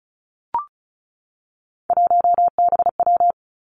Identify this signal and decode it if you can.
Morse code, heard as a keyed tone, about 36 wpm, 700 Hz. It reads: 1BW
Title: Three Challenging Letter-Number Combinations 35wpm